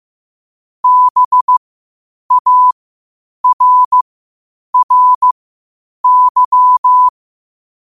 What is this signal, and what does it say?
Morse code, heard as a keyed tone, about 15 wpm, spaced as for 8 wpm, 1000 Hz. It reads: BARRY
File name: 8wpm